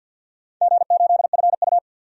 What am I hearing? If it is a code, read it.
Morse code, heard as a keyed tone, about 53 wpm, 700 Hz. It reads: G8FU
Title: DXCC-100 Call Signs - 4 Characters - 50wpm